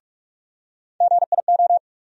Morse code, heard as a keyed tone, about 46 wpm, 700 Hz. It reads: GIO
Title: USA Call Sign Postfixes - 3 Letters 45wpm